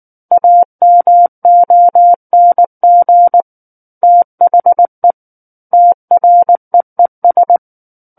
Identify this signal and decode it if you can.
Morse code, heard as a keyed tone, about 19 wpm, 700 Hz. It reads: AMONG THE TREES
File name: Book One / 10-Chapter 10 - 19 WPM